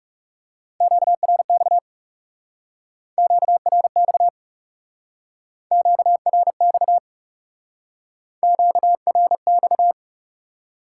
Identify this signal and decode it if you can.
Morse code, heard as a keyed tone, about 38 wpm, 700 Hz. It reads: QRX QRX QRX QRX
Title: Speed-Racing Top 180 QSO Elements 30wpm